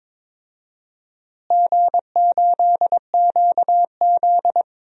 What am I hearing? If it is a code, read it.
Morse code, heard as a keyed tone, about 21 wpm, 700 Hz. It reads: G8QZ
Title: DXCC-100 Call Signs - 4 Characters - Repeated 3x - 22wpm